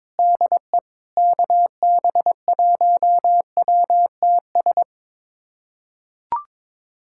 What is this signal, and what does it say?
Morse code, heard as a keyed tone, about 23 wpm, 700 Hz. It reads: DE KB1WTS
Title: USA Callsign DE Callsign - 6 Characters - 22wpm